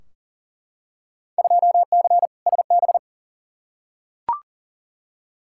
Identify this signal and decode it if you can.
Morse code, heard as a keyed tone, about 40 wpm, 700 Hz. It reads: 2C SB
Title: Field Day Exchanges Encore - 40wpm